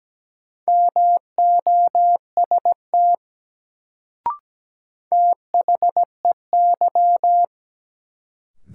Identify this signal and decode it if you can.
Morse code, heard as a keyed tone, about 17 wpm, 700 Hz. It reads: MOST THEY
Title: Sets of 2 Words - New Word Drill - 17wpm